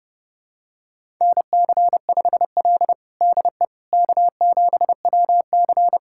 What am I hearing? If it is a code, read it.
Morse code, heard as a keyed tone, about 30 wpm, 700 Hz. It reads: NC5L DE K7WC
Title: USA Callsign DE Callsign - 4 Characters - 30wpm